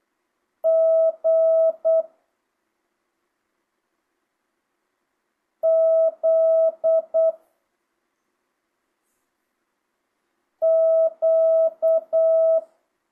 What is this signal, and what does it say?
Morse code, heard as a keyed tone, about 8 wpm, 650 Hz. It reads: G Z Q
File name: jh8cbh / 2025cw